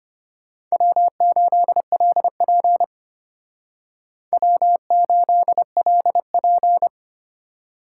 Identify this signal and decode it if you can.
Morse code, heard as a keyed tone, about 33 wpm, 700 Hz. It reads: W8LP W8LP